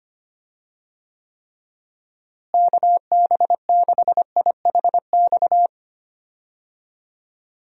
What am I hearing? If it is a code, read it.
Morse code, heard as a keyed tone, about 25 wpm, 700 Hz. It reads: KB6IHX
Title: USA Call Signs - 6 Characters - Repeated 3x - 25wpm